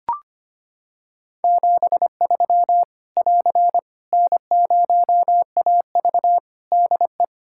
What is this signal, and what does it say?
Morse code, heard as a keyed tone, about 25 wpm, 700 Hz. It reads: 73 <AR> N0AV DE
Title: Practice QSO - Exchange 4 - Ending - 25wpm